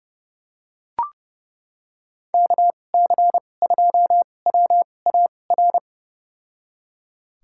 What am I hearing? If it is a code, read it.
Morse code, heard as a keyed tone, about 30 wpm, spaced as for 22 wpm, 700 Hz. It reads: KC2WAR
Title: USA Call Signs - 6 Characters - 2x Character Spacing - 30wpm